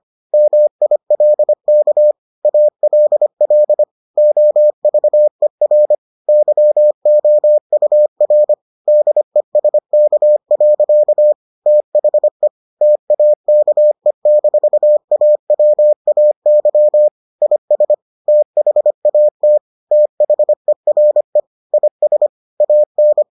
Morse code, heard as a keyed tone, about 25 wpm, 600 Hz. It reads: MILK ALL OVER YOUR DESK. THE TAKE-AWAY IS THAT THERE IS AN